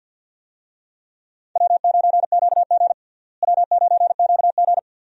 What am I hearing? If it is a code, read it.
Morse code, heard as a keyed tone, about 52 wpm, 700 Hz. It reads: W9QG W9QG
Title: USA Call Signs - 4 Characters - Repeated 2x - 50wpm